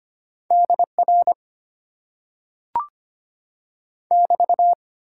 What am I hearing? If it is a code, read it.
Morse code, heard as a keyed tone, about 25 wpm, 700 Hz. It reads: DR <BT>
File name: Top 90 QSO Elements - 8x Character Spacing - 25wpm